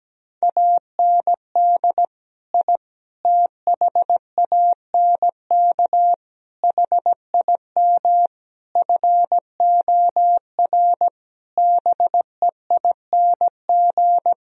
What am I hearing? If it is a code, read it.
Morse code, heard as a keyed tone, about 17 wpm, 700 Hz. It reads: AND I THANK HIM FOR BEING
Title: Sentences from Top 1000 Words - Review Part 3 - 17wpm